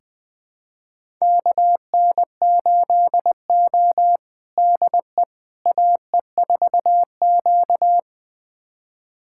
Morse code, heard as a keyed tone, about 20 wpm, 700 Hz. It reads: KN8O DE AE4Q
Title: USA Callsign DE Callsign - 4 Characters - 20wpm